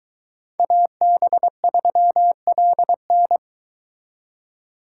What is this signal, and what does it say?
Morse code, heard as a keyed tone, about 22 wpm, 700 Hz. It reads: AB3LN